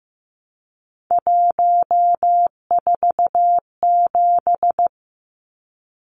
Morse code, heard as a keyed tone, about 15 wpm, 700 Hz. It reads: 147
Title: Numbers - 3-Digit 15wpm